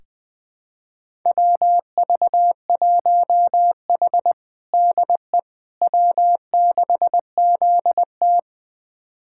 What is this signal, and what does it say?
Morse code, heard as a keyed tone, about 20 wpm, 700 Hz. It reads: WV1H DE W6ZT
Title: USA Callsign DE Callsign - 4 Characters - 20wpm